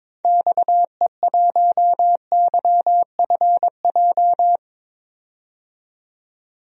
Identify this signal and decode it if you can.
Morse code, heard as a keyed tone, about 22 wpm, 700 Hz. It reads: XE1YFJ
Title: DXCC-100 Call Signs - 6 Characters - 22wpm